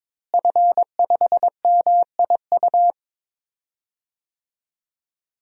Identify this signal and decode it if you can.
Morse code, heard as a keyed tone, about 22 wpm, 700 Hz. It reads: F5MIU
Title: DXCC-100 Call Signs - 5 Characters - Repeated 2x - 22wpm